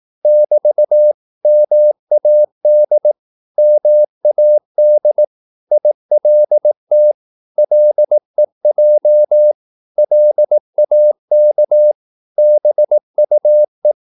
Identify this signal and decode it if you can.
Morse code, heard as a keyed tone, about 18 wpm, 600 Hz. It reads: <BT> MAD MAD ILT LEJ LAK BUE
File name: Korte ord DK 18wpm